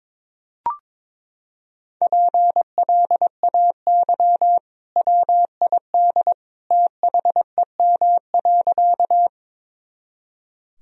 Morse code, heard as a keyed tone, about 22 wpm, 700 Hz. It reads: PLAY WID THEM.
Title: Sentences from Top 200 Words and Abbreviations 22wpm